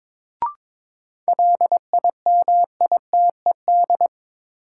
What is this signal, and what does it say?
Morse code, heard as a keyed tone, about 22 wpm, 700 Hz. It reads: LIMITED
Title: Sentences from Top 900 Words - New Word Drill - 22wpm